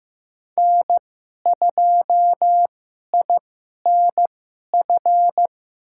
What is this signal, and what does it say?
Morse code, heard as a keyed tone, about 15 wpm, spaced as for 11 wpm, 700 Hz. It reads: N2INF
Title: USA Call Signs - 5 Characters - 2x Character Spacing - 15wpm